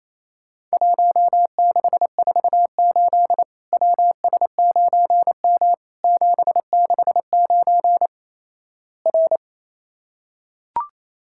Morse code, heard as a keyed tone, about 28 wpm, 700 Hz. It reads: 1648 WS9M 769 R
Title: Worked All Europe DX Contest - QTC Exchange - 28wpm